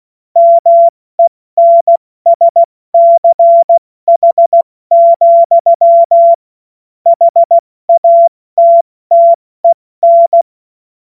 Morse code, heard as a keyed tone, about 16 wpm, spaced as for 14 wpm, 680 Hz. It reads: MENSCH, HATTEN
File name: Morsekurs / Witze / Witz 19. Oktober 2021, 70 F80